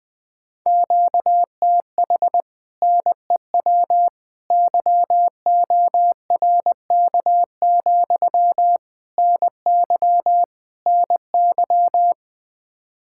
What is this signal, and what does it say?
Morse code, heard as a keyed tone, about 20 wpm, 700 Hz. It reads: QTH NEW YORK, NY NY
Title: Speed-Racing Common QSO Phrases Encore - 20wpm